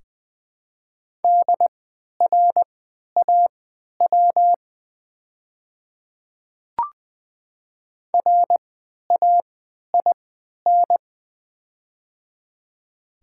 Morse code, heard as a keyed tone, about 20 wpm, spaced as for 11 wpm, 700 Hz. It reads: DRAW RAIN
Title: 3x Character Spacing - Top 300 Words - 20wpm